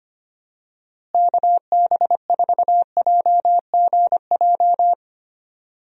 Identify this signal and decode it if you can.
Morse code, heard as a keyed tone, about 25 wpm, 700 Hz. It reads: KB4JGJ
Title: USA Call Signs - 6 Characters - Repeated 3x - 25wpm